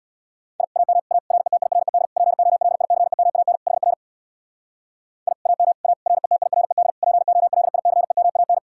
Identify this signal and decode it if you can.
Morse code, heard as a keyed tone, about 60 wpm, 700 Hz. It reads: I AM A RETIRED POLICEMAN HR I AM A RETIRED POLICEMAN
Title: Speed-Racing Common QSO Phrases - 50wpm